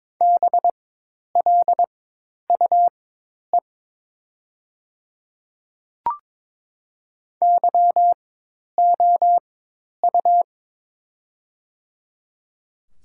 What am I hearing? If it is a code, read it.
Morse code, heard as a keyed tone, about 21 wpm, spaced as for 10 wpm, 700 Hz. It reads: BLUE YOU
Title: 4x Character Spacing - Top 500 Words - 22wpm